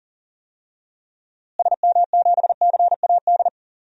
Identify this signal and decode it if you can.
Morse code, heard as a keyed tone, about 40 wpm, 700 Hz. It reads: SM7CAD
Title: DXCC-100 Call Signs - 6 Characters - Repeated 2x - 40wpm